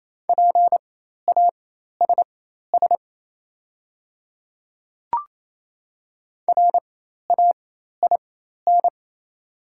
Morse code, heard as a keyed tone, about 28 wpm, spaced as for 13 wpm, 690 Hz. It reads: PASS RAIN
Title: 4x Character Spacing - Top 500 Words - 28wpm